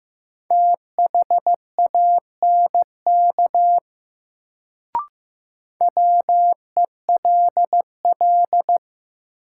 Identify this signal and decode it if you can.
Morse code, heard as a keyed tone, about 15 wpm, 700 Hz. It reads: THANK WELL
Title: Sets of 2 Words - New Word Drill - 15wpm